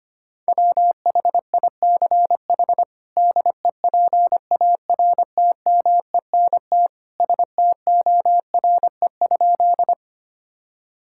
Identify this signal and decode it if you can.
Morse code, heard as a keyed tone, about 25 wpm, 700 Hz. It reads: WHICH DEPARTMENT STORE?
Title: Sentences from Top 1000 Words - Review Part 5 - 25wpm